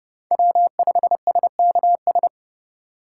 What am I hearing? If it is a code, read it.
Morse code, heard as a keyed tone, about 30 wpm, 700 Hz. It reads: W5SKS